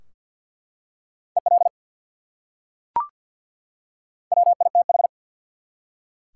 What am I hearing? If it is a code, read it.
Morse code, heard as a keyed tone, about 47 wpm, 700 Hz. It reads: EL WITH
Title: Top 180 QSO Elements 50wpm